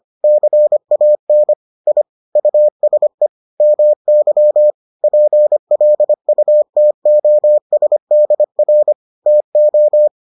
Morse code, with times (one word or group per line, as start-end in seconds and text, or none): 0.2-1.5 CAN
1.9-2.0 I
2.4-3.3 USE
3.6-4.7 MY
5.0-8.9 PLUTOSDR
9.3-10.1 TO